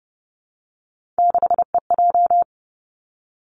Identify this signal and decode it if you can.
Morse code, heard as a keyed tone, about 30 wpm, 700 Hz. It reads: 6EJ
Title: Three Challenging Letter-Number Combinations 30wpm